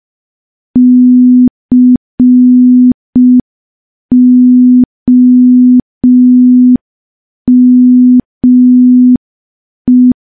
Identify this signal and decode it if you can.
Morse code, heard as a keyed tone, about 5 wpm, 250 Hz. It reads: COME
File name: morse / morse-training-05WPM